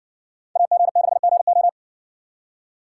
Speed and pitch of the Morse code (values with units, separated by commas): 60 wpm, 700 Hz